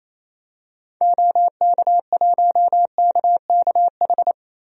Morse code, heard as a keyed tone, about 28 wpm, 700 Hz. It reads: OK1KKH
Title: DXCC-100 Call Signs - 6 Characters - 28wpm